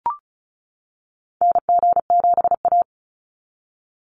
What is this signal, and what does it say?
Morse code, heard as a keyed tone, about 34 wpm, 700 Hz. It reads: NG7A
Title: NA SOTA Activator Callsigns 35wpm